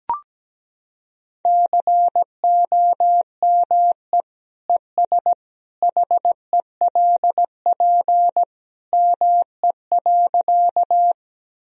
Morse code, heard as a keyed tone, about 17 wpm, 700 Hz. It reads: COME ES HELP ME.
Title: Sentences from Top 200 Words and Abbreviations 17wpm